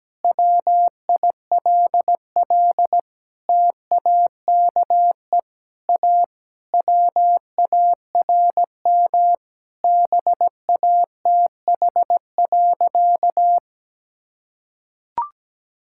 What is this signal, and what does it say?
Morse code, heard as a keyed tone, about 17 wpm, 700 Hz. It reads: WILL TAKE A WARM BATH.
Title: Sentences from Top 1600 Words 17wpm